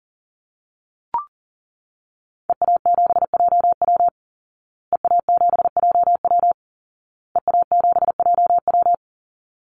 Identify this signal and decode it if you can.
Morse code, heard as a keyed tone, about 40 wpm, 700 Hz. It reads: EA7JW EA7JW EA7JW
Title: Call Signs Repeated 2x - 40wpm